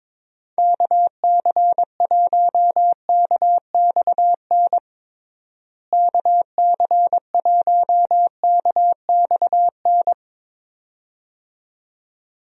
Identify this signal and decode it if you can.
Morse code, heard as a keyed tone, about 22 wpm, 700 Hz. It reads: KC1KXN KC1KXN
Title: USA Call Signs - 6 Characters - Repeated 2x - 22wpm